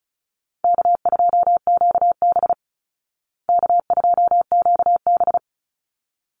Morse code, heard as a keyed tone, about 34 wpm, 700 Hz. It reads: K2QB K2QB
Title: Easy Call Signs Repeated 2x - 35wpm